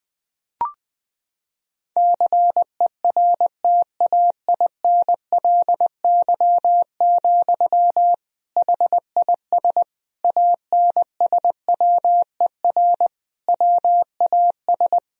Morse code, heard as a keyed tone, about 20 wpm, 700 Hz. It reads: CERTAINLY, HIS ANSWER WAS
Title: Sentences from Top 1000 Words - Review Part 2 - 20wpm